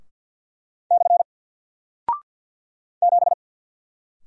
Morse code, heard as a keyed tone, about 48 wpm, 700 Hz. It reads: / 7
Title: Single Character (Letter:Number:Punctuation:Prosign) Alt 50wpm